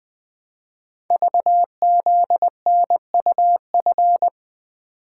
Morse code, heard as a keyed tone, about 20 wpm, 700 Hz. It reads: VZNUF
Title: Code Groups - 5 Letter:Numbers - 20wpm